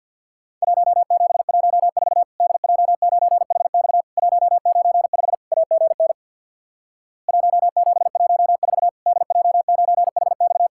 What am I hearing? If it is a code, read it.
Morse code, heard as a keyed tone, about 50 wpm, 700 Hz. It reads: 1714 DJ9HX 195 AGN 1714 DJ9HX
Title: Worked All Europe DX Contest - QTC Exchange with Repeats - 50wpm